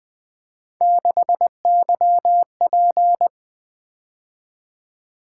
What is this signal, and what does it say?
Morse code, heard as a keyed tone, about 20 wpm, 700 Hz. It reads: 6YP
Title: Three Letter-Number Combinations - 20wpm